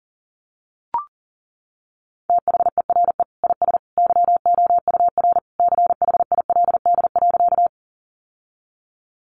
Morse code, heard as a keyed tone, about 40 wpm, 690 Hz. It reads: THERE IS YOUR CHILD.
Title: Sentences from Top 800 Words 40wpm